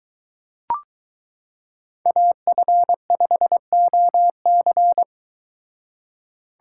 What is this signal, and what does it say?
Morse code, heard as a keyed tone, about 23 wpm, 700 Hz. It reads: AF5OC